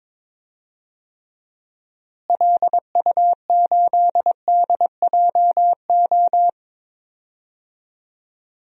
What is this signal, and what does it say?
Morse code, heard as a keyed tone, about 22 wpm, 700 Hz. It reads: LU8DJO
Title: DXCC-100 Call Signs - 6 Characters - Repeated 3x - 22wpm